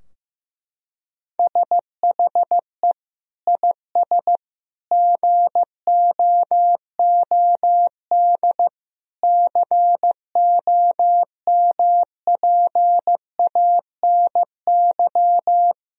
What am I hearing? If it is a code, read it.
Morse code, heard as a keyed tone, about 14 wpm, 700 Hz. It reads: SHE IS GOOD COMPANY
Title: Sentences from Top 1000 Words - Review Part 3 - 15wpm